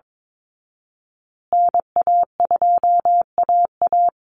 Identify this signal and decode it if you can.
Morse code, heard as a keyed tone, about 22 wpm, 700 Hz. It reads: NA2AA
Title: Call Signs Repeated 2x - 22wpm